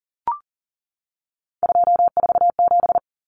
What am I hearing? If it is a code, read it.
Morse code, heard as a keyed tone, about 40 wpm, 700 Hz. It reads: 247
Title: Numbers - 3-Digit 40wpm